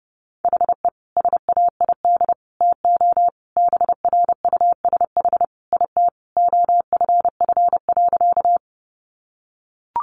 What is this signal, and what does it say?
Morse code, heard as a keyed tone, about 30 wpm, 700 Hz. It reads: HE SAID TO BRUSH IT OFF.
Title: Sentences from Top 1100 Words 30wpm